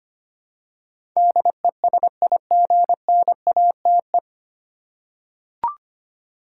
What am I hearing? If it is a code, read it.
Morse code, heard as a keyed tone, about 25 wpm, 700 Hz. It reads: DESIGNATE
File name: practice / 25wpm / 9 Letter Words - First 3 Letters Given - 25wpm